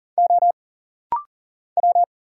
Morse code, heard as a keyed tone, about 40 wpm, 700 Hz. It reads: O W
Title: Single Letter-Number Alt - Rapid-Fire 40wpm